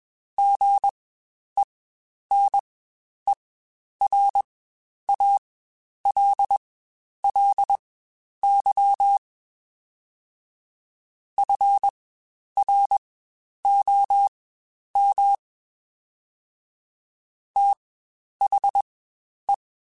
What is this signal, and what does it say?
Morse code, heard as a keyed tone, about 21 wpm, spaced as for 10 wpm, 800 Hz. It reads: GENERALLY FROM THE